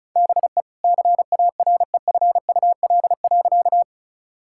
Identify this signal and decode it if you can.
Morse code, heard as a keyed tone, about 34 wpm, 700 Hz. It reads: BE CAREFUL.
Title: Sentences from Top 1600 Words 35wpm